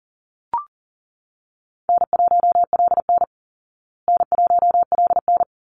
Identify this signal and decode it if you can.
Morse code, heard as a keyed tone, about 38 wpm, 700 Hz. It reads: N1LN N1LN
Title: Easy Call Signs Repeated 2x - 40wpm